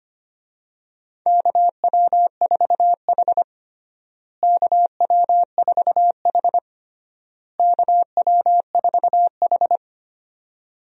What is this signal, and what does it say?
Morse code, heard as a keyed tone, about 25 wpm, 700 Hz. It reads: KW4H KW4H KW4H
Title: USA Call Signs - 4 Characters - Repeated 3x - 25wpm